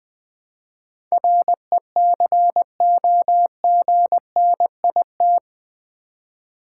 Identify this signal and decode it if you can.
Morse code, heard as a keyed tone, about 20 wpm, 700 Hz. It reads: RECOGNIT
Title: Build A Word - 4 Letters At A Time - 11-Letter Words - 20wpm